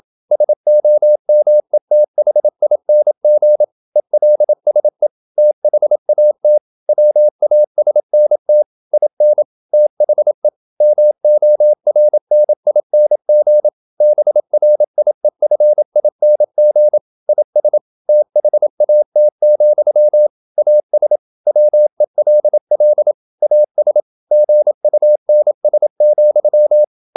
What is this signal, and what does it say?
Morse code, heard as a keyed tone, about 27 wpm, 600 Hz. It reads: SOMETHING ELSE THAT WASNT IN THE MORNING BRIEFING IS THAT, AS WELL AS GUNS,
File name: Never 0027wpm